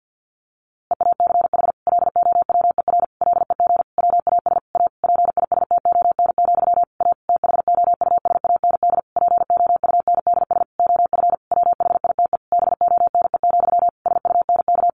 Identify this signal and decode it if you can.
Morse code, heard as a keyed tone, about 50 wpm, 700 Hz. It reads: EACH LOWER LEG WAS A PISTON, A THOUSAND POUNDS OF WHITE BONE, SUNK
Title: A Sound Of Thunder 50wpm